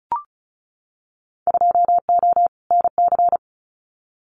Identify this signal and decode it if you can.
Morse code, heard as a keyed tone, about 36 wpm, 700 Hz. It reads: 2O NC
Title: Winter Field Day Exchanges 35wpm